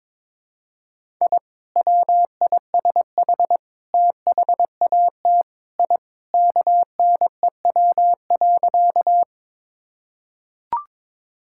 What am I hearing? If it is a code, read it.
Morse code, heard as a keyed tone, about 22 wpm, 700 Hz. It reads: I WISH THAT I KNEW.
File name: practice / 22wpm / Sentences with 2-Word Phrases - 22wpm